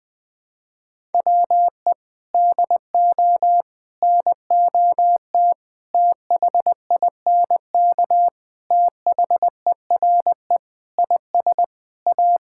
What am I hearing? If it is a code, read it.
Morse code, heard as a keyed tone, about 21 wpm, 700 Hz. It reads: WE DO NOT THINK THERE IS A
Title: Sentences with 4-Word Phrases - 20wpm